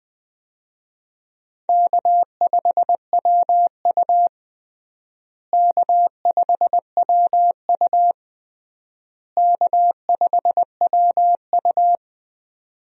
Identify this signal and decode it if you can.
Morse code, heard as a keyed tone, about 20 wpm, 700 Hz. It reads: K5WU K5WU K5WU
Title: USA Call Signs - 4 Characters - Repeated 3x - 20wpm